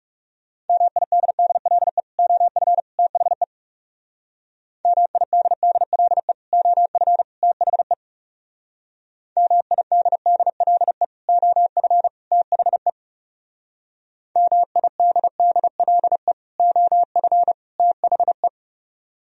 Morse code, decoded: MIDDLE OF THE MIDDLE OF THE MIDDLE OF THE MIDDLE OF THE

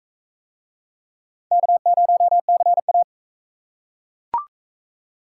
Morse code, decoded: K0CA

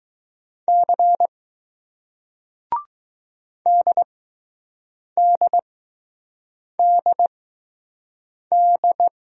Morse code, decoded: C DDDD

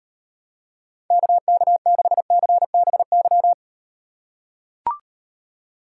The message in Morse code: KK6CBY

